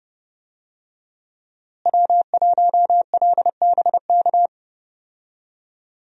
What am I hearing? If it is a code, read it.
Morse code, heard as a keyed tone, about 30 wpm, 700 Hz. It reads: W1LBK
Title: USA Call Signs - 5 Characters - Repeated 3x - 30wpm